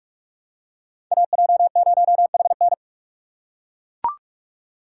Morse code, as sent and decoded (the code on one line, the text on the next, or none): .- .--- ----- .... -.
AJ0HN